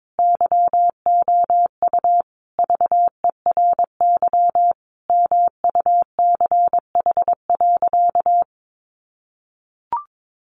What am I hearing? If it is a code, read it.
Morse code, heard as a keyed tone, about 22 wpm, 700 Hz. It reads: YOU VERY MUCH.
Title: Sentences from Top 600 Words 22wpm